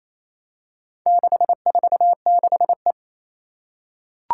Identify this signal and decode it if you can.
Morse code, heard as a keyed tone, about 27 wpm, 700 Hz. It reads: 646E
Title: Four Challenging Letter-Number Combinations 28wpm